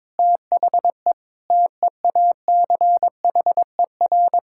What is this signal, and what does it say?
Morse code, heard as a keyed tone, about 22 wpm, 700 Hz. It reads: THE TEACHER
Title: Sentences from Top 1000 Words - Review Part 6 - 22wpm